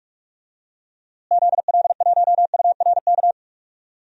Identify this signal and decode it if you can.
Morse code, heard as a keyed tone, about 46 wpm, 700 Hz. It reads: ZL1URK
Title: DXCC-100 Call Signs - 6 Characters - Repeated 3x - 45wpm